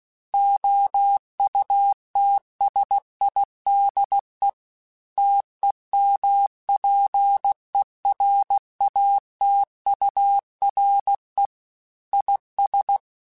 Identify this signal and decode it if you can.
Morse code, heard as a keyed tone, about 16 wpm, 800 Hz. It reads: OUTSIDE TEMPERATURE IS
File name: Weather Sets / Weather Sets-mp3s / Weather 4 17 @15